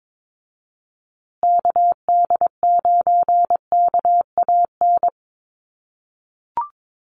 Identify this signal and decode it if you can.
Morse code, heard as a keyed tone, about 22 wpm, 700 Hz. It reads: KD9KAN
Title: USA Call Signs Repeated 3x 22wpm